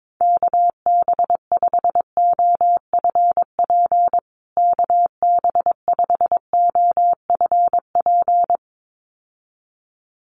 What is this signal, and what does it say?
Morse code, heard as a keyed tone, about 22 wpm, 700 Hz. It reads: KB5OFP KB5OFP
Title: USA Call Signs Repeated 2x 22wpm